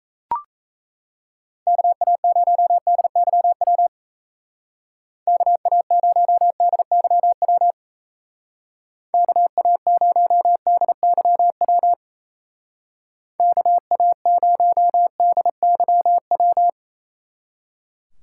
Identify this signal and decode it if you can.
Morse code, heard as a keyed tone, about 34 wpm, 700 Hz. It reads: KA0DYW KA0DYW KA0DYW KA0DYW
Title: Speed Racing - USA Call Signs 28wpm